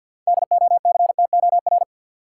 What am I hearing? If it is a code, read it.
Morse code, heard as a keyed tone, about 51 wpm, 700 Hz. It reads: DOCTOR
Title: Sentences from Top 1500 Words - New Word Drill - 50wpm